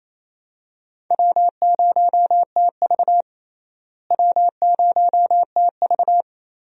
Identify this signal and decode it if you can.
Morse code, heard as a keyed tone, about 27 wpm, 700 Hz. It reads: W0TV W0TV
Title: USA Call Signs - 4 Characters - Repeated 3x - 28wpm